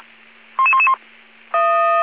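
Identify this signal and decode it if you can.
Five-tone selective call (ZVEI), heard as quick runs of five tones, tones of about 70 ms. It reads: 11111
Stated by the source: ZVEI five-tone call 55555 received on SDR.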